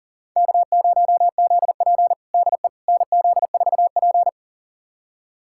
Morse code, heard as a keyed tone, about 40 wpm, 700 Hz. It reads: K0ZP DE NZ4P
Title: USA Callsign DE Callsign - 4 Characters - 40wpm